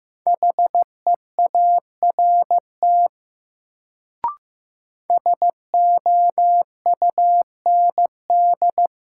Morse code, heard as a keyed tone, about 15 wpm, 700 Hz. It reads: HEART SOUND
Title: Sets of 3 Words - New Word Drill - 15wpm